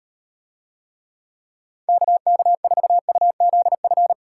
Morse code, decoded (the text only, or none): KK4UZF